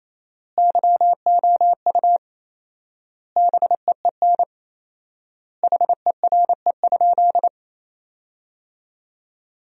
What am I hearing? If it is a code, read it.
Morse code, heard as a keyed tone, about 28 wpm, 700 Hz. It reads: YOU BEEN HERE?
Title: 4x Word-Spacing - Sentences from Top 200 Words - 28wpm